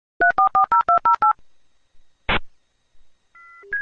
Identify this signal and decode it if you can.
DTMF, heard as telephone touch tones, tones of about 120 ms. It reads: 3440208